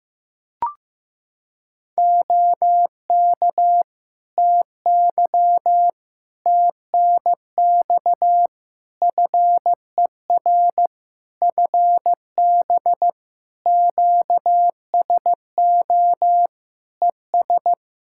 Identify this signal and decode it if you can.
Morse code, heard as a keyed tone, about 15 wpm, 700 Hz. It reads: OK TY TNX FER FB QSO ES
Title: K1USN SST QSO Practice 15wpm